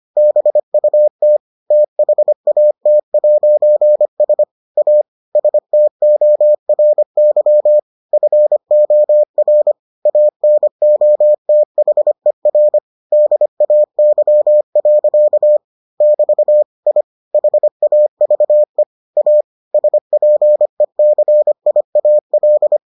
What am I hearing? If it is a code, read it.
Morse code, heard as a keyed tone, about 25 wpm, 600 Hz. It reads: BUT THAT'S A STORY FOR ANOTHER DAY. <BT> I HAVE A SPECIAL